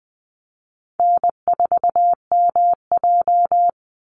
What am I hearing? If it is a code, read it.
Morse code, heard as a keyed tone, about 20 wpm, 700 Hz. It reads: N4MJ
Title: NA SOTA Chaser Callsigns 20wpm